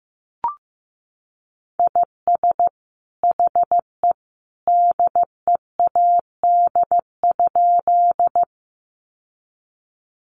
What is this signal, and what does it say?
Morse code, heard as a keyed tone, about 15 wpm, 700 Hz. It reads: IS HE DEAD?
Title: Sentences from Top 1500 Words 15wpm